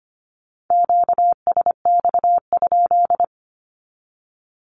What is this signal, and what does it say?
Morse code, heard as a keyed tone, about 25 wpm, 700 Hz. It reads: QSX?
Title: Q-Codes 25wpm